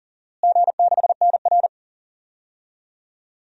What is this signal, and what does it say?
Morse code, heard as a keyed tone, about 40 wpm, 700 Hz. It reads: G6NR
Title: DXCC-100 Call Signs - 4 Characters - Repeated 3x - 40wpm